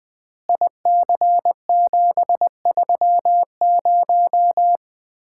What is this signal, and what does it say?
Morse code, heard as a keyed tone, about 20 wpm, 700 Hz. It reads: IC730
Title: Common QSO Phrases - 4x Word-Spacing - 20wpm